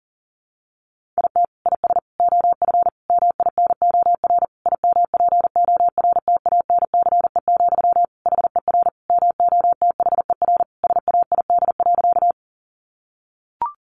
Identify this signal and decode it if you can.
Morse code, heard as a keyed tone, about 40 wpm, 700 Hz. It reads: IT IS OF MINOR IMPORTANCE, HER MOTHER SAID.
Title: Sentences from Top 1500 Words 40wpm